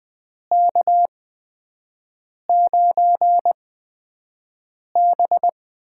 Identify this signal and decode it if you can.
Morse code, heard as a keyed tone, about 20 wpm, spaced as for 5 wpm, 700 Hz. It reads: K9B
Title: USA Call Signs - 4 Characters - 8x Character Spacing - 20wpm